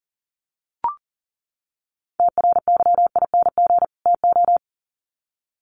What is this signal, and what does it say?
Morse code, heard as a keyed tone, about 37 wpm, 700 Hz. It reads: TRYING TO
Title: Sets of 2 Words 40wpm